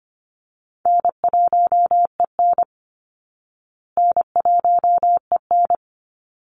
Morse code, heard as a keyed tone, about 25 wpm, 700 Hz. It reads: N1EN N1EN
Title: Call Signs Repeated 3x - 25wpm